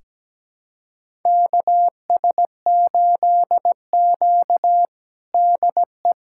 Morse code, heard as a keyed tone, about 17 wpm, 700 Hz. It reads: KS8Q DE
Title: USA Callsign DE Callsign - 4 Characters - 17wpm